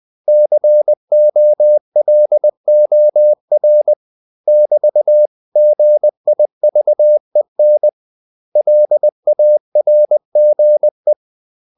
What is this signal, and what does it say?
Morse code, heard as a keyed tone, about 20 wpm, 600 Hz. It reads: COLOR <BT>GIVEN LARGE
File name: CW / Practice / 500MostCommonEnglishWords20